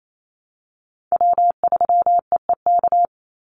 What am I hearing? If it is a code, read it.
Morse code, heard as a keyed tone, about 28 wpm, 700 Hz. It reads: W3EEK
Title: Call Signs Repeated 3x - 28wpm